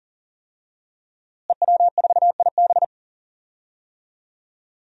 Morse code, heard as a keyed tone, about 40 wpm, 700 Hz. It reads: EW4IB
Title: DXCC-100 Call Signs - 5 Characters - Repeated 3x - 40wpm